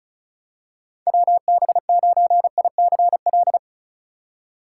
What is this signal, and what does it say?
Morse code, heard as a keyed tone, about 34 wpm, 700 Hz. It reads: WB9ICL
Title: USA Call Signs - 6 Characters - 4x Character Spacing - 35wpm